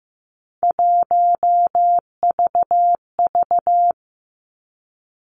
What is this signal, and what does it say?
Morse code, heard as a keyed tone, about 15 wpm, 700 Hz. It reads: 1VV
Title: Three Challenging Letter-Number Combinations 15wpm